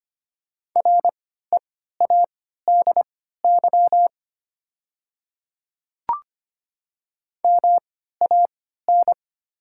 Morse code, heard as a keyed tone, about 25 wpm, spaced as for 14 wpm, 690 Hz. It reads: READY MAN
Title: 3x Character Spacing - Top 500 Words - 25wpm